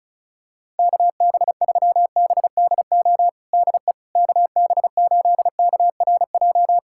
Morse code, heard as a keyed tone, about 36 wpm, 700 Hz. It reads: KB3BDO DE KB8KRJ
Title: USA Callsign DE Callsign 35wpm